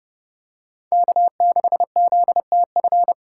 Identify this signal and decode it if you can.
Morse code, heard as a keyed tone, about 30 wpm, 700 Hz. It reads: K6ZTF
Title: USA Call Signs - 5 Characters - 6x Character Spacing - 30wpm